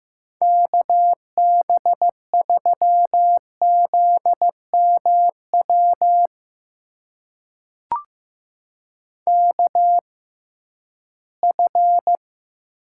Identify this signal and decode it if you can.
Morse code, heard as a keyed tone, about 15 wpm, spaced as for 11 wpm, 700 Hz. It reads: KB3ZMW KF
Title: USA Call Signs - 6 Characters - 6x Character Spacing - 15wpm